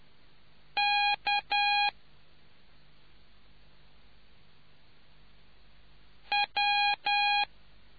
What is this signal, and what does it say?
Morse code, heard as a keyed tone, about 10 wpm, 820 Hz. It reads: K W